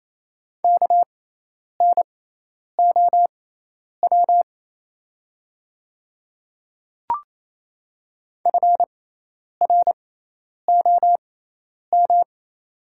Morse code, heard as a keyed tone, about 28 wpm, spaced as for 10 wpm, 700 Hz. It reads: KNOW FROM
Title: 6x Character Spacing - Top 100 Words - 28wpm